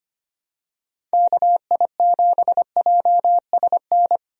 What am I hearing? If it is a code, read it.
Morse code, heard as a keyed tone, about 25 wpm, 700 Hz. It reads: KI7JSN
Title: USA Call Signs - 6 Characters - Repeated 3x - 25wpm